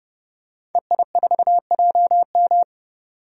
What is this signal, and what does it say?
Morse code, heard as a keyed tone, about 32 wpm, 700 Hz. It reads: EI4JM
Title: DXCC-100 Call Signs - 5 Characters - 30wpm